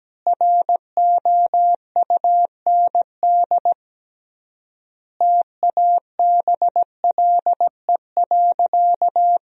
Morse code, heard as a keyed tone, about 17 wpm, 700 Hz. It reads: ROUND TABLE.
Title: 3x Word-Spacing - Sentences from Top 400 Words - 17wpm